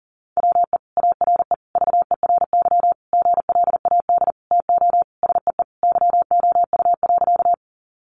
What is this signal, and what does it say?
Morse code, heard as a keyed tone, about 40 wpm, 700 Hz. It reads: WE ARE VERY GLAD TO SEE YOU.
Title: Sentences from Top 800 Words 40wpm